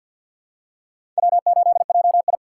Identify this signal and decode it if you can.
Morse code, heard as a keyed tone, about 46 wpm, 700 Hz. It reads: W8JI
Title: USA Call Signs - 4 Characters - 2x Character Spacing - 50wpm